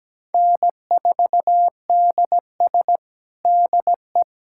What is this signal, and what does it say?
Morse code, heard as a keyed tone, about 17 wpm, 700 Hz. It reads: N4DS DE
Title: USA Callsign DE Callsign - 4 Characters - 17wpm